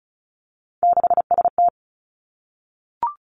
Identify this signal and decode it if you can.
Morse code, heard as a keyed tone, about 36 wpm, 700 Hz. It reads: 6ST
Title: Three Challenging Letter-Number Combinations 35wpm